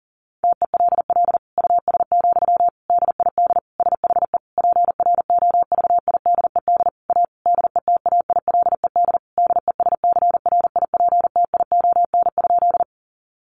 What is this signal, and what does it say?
Morse code, heard as a keyed tone, about 37 wpm, 700 Hz. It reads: TELL US, DID SHE PROVIDED A DETAILED DESCRIPTION?
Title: Sentences from Top 1500 Words 40wpm